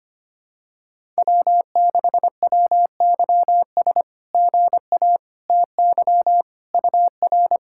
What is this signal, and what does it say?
Morse code, heard as a keyed tone, about 25 wpm, 700 Hz. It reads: W6WYS GA TY UR
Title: SOTA QSO Practice 25wpm